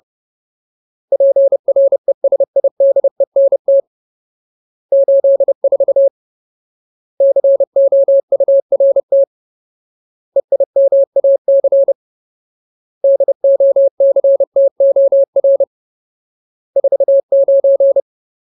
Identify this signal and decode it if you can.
Morse code, heard as a keyed tone, about 30 wpm, 550 Hz. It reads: PRESIDENT 84 COURT EIMAC DOCTOR 49
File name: Ultimate Word List / 30WPM Ultimate